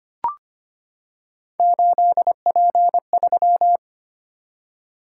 Morse code, heard as a keyed tone, about 25 wpm, 700 Hz. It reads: 8P3
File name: Three Letter-Number Combinations - 25wpm